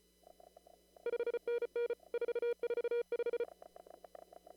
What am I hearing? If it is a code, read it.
Morse code, heard as a keyed tone, about 33 wpm, 2350 Hz, with noise. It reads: 5NN 445
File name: CW / kt5j wpx cw-op-k5pi